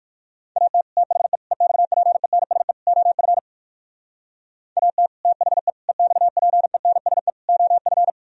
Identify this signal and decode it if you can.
Morse code, heard as a keyed tone, about 61 wpm, 700 Hz. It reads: AT THE EXPENSE OF AT THE EXPENSE OF